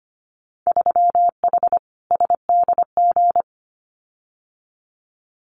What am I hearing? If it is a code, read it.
Morse code, heard as a keyed tone, about 25 wpm, 700 Hz. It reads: 3H SDG
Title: Winter Field Day Exchanges Encore 25wpm